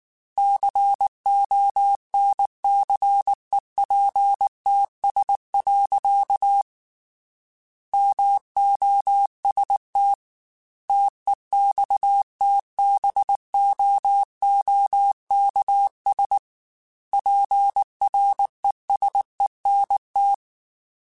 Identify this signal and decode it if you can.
Morse code, heard as a keyed tone, about 19 wpm, 800 Hz. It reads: CONCEPTS. MOST TEXTBOOKS PRESENT
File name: Technical Subjects / Technical Subjects-mp3s / Gates Intro @18